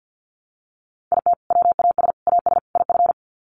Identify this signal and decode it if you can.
Morse code, heard as a keyed tone, about 48 wpm, 700 Hz. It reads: IT WAS AS IF
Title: Sets of 4 Words 50wpm